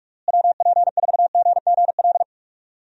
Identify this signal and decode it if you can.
Morse code, heard as a keyed tone, about 41 wpm, 700 Hz. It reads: WP4GGL
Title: USA Call Signs - 6 Characters - Repeated 3x - 45wpm